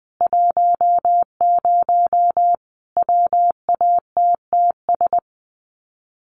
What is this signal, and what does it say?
Morse code, heard as a keyed tone, about 20 wpm, 700 Hz. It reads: 10 WATTS
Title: Common QSO Phrases 20wpm